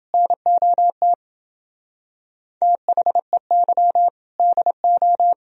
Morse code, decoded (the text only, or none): NOT THEY DO